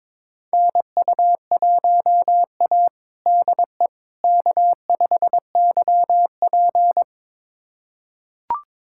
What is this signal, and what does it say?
Morse code, heard as a keyed tone, about 22 wpm, 700 Hz. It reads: NU1A DE K5YP
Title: USA Callsign DE Callsign - 4 Characters - 22wpm